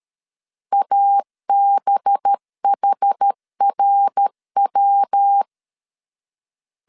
Morse code, heard as a keyed tone, about 13 wpm, 800 Hz. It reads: ABHRW
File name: LEZIONI / 6 PKX0 / 6 PKX0 12wpm 2231